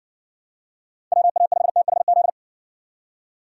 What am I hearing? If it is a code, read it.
Morse code, heard as a keyed tone, about 54 wpm, 700 Hz. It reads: WA5THZ